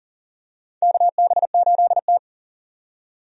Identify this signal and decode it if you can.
Morse code, heard as a keyed tone, about 40 wpm, 700 Hz. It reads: KB8T
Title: USA Call Signs - 4 Characters - 40wpm